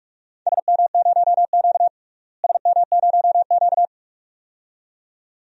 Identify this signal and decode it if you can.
Morse code, heard as a keyed tone, about 43 wpm, 700 Hz. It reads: SM0Q SM0Q
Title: DXCC-100 Call Signs - 4 Characters - Repeated 2x - 45wpm